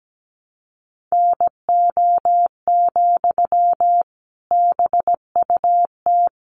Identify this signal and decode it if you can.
Morse code, heard as a keyed tone, about 17 wpm, 700 Hz. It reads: NO, BUT
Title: Sentences from Top 1100 Words 17wpm